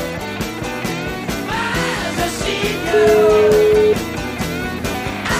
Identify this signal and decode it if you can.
Ringback tone (Europe, 425 Hz), with noise.